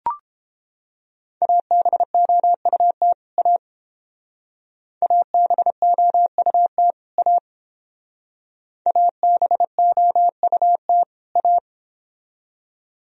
Morse code, heard as a keyed tone, about 29 wpm, 700 Hz. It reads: ABOUT A ABOUT A ABOUT A